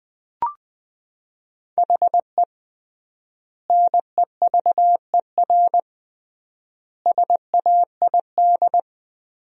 Morse code, heard as a keyed tone, about 20 wpm, 700 Hz. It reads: HE NEVER SAID
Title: 3x Word-Spacing - Sentences from Top 500 Words - 20wpm